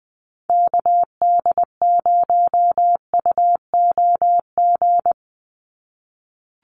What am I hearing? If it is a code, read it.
Morse code, heard as a keyed tone, about 20 wpm, 700 Hz. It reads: KD0UOG
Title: USA Call Signs 20wpm